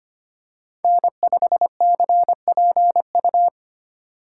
Morse code, heard as a keyed tone, about 25 wpm, 690 Hz. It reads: N5CPU
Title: USA Call Signs - 5 Characters - 6x Character Spacing - 25wpm